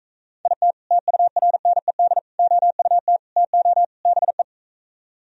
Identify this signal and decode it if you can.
Morse code, heard as a keyed tone, about 44 wpm, 700 Hz. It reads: IT TURNED OUT TO BE